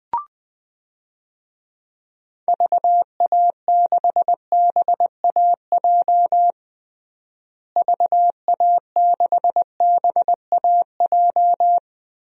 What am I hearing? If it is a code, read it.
Morse code, heard as a keyed tone, about 20 wpm, 700 Hz. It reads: VA6BAJ VA6BAJ
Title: DXCC-100 Call Signs - 6 Characters - Repeated 3x - 20wpm